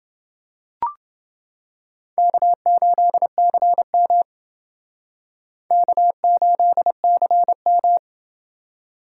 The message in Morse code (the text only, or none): K8CM K8CM